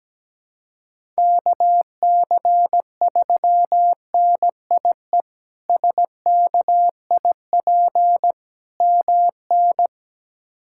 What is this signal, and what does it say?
Morse code, heard as a keyed tone, about 17 wpm, 700 Hz. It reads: KC3NIE SKIP MN
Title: K1USN SST QSO Practice - 17wpm